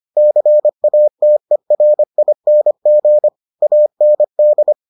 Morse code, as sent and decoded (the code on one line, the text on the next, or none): -.-. .- - . .-. .. -. --. / .- -. -..
CATERING AND